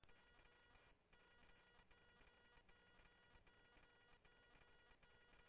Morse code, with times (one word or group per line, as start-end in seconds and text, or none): none